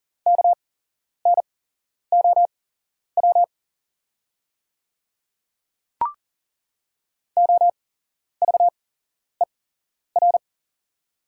Morse code, heard as a keyed tone, about 39 wpm, spaced as for 11 wpm, 700 Hz. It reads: KNOW OVER
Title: 8x Character Spacing - Top 100 Words - 40wpm